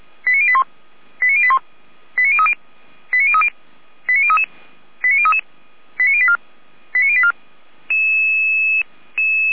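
Five-tone selective call (ZVEI), heard as quick runs of five tones, tones of about 70 ms. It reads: 89081 89081 89030 89030 89033 89033 89084 89084
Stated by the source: Five-tone ZVEI selective call received by SDR.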